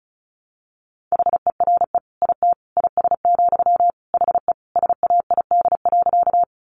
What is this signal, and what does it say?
Morse code, heard as a keyed tone, about 34 wpm, 700 Hz. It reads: HERE IT IS, HE SAID.
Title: Sentences from Top 200 Words 35wpm